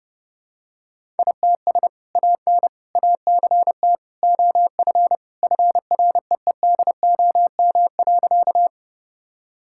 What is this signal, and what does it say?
Morse code, heard as a keyed tone, about 30 wpm, 700 Hz. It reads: ITS AN ACT OF FREEDOM.
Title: Sentences from Top 1000 Words - Review Part 5 - 30wpm